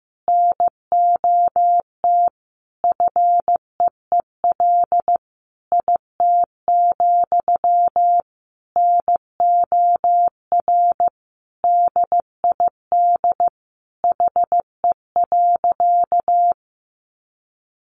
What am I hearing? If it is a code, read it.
Morse code, heard as a keyed tone, about 15 wpm, 700 Hz. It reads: NOT FEEL IT, NOR DID HE.
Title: Sentences from Top 1500 Words 15wpm